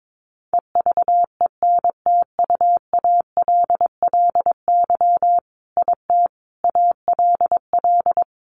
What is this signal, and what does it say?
Morse code, heard as a keyed tone, about 22 wpm, 700 Hz. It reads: EVENTUALLY IT ALL
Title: A Sound Of Thunder 22wpm